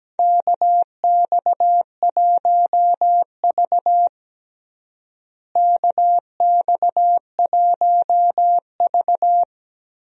Morse code, heard as a keyed tone, about 17 wpm, 700 Hz. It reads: KX1V KX1V
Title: USA Call Signs - 4 Characters - Repeated 3x - 17wpm